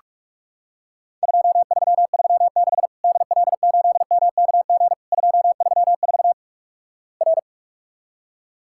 Morse code, heard as a keyed tone, about 46 wpm, 700 Hz. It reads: 2336 DL8MKG 234 R
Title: Worked All Europe DX Contest - QTC Exchange - 45wpm